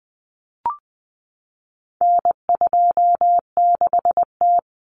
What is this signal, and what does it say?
Morse code, heard as a keyed tone, about 20 wpm, 700 Hz. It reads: N26T
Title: Common Cut Numbers - 4-Digit 20wpm